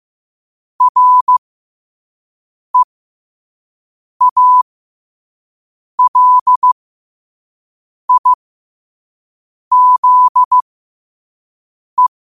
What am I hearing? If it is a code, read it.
Morse code, heard as a keyed tone, about 15 wpm, spaced as for 5 wpm, 1000 Hz. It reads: REALIZE
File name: qotdfiles / 5wpm